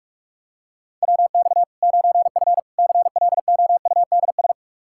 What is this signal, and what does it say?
Morse code, heard as a keyed tone, about 44 wpm, 700 Hz. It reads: WX 9F CLOUDS
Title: Practice QSO - Exchange 2 - Info - 45wpm